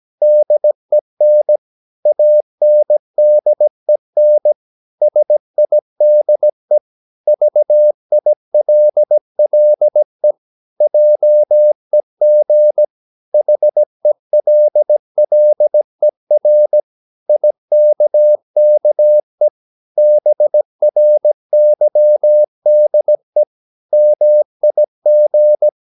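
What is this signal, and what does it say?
Morse code, heard as a keyed tone, about 17 wpm, 600 Hz. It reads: DEN ANDEN SIDE VILLE JEG HELLER IKKE BRYDE MIG
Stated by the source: Slow (QRS) Morse practice at 17 wpm, from CW med Gnister.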